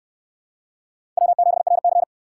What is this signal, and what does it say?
Morse code, heard as a keyed tone, about 68 wpm, 700 Hz. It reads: W7RC